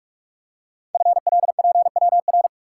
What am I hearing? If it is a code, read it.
Morse code, heard as a keyed tone, about 44 wpm, 700 Hz. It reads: FLPWR
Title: Code Groups - 5 Letters - 45wpm